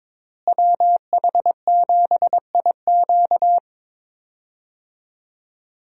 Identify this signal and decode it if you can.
Morse code, heard as a keyed tone, about 22 wpm, 700 Hz. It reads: WH7IQ
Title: USA Call Signs - 5 Characters - Repeated 3x - 22wpm